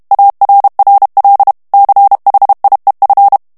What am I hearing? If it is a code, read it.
Morse code, heard as a keyed tone, about 32 wpm, 800 Hz. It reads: ARRL CHIEF
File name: morse mp3 / ANGLAIS texte clair 3 vitesse 30 mots minute